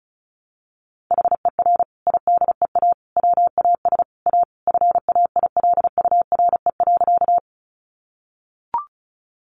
Morse code, heard as a keyed tone, about 34 wpm, 700 Hz. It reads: HER IDEA WAS A FAILURE.
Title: Sentences from Top 900 Words 35wpm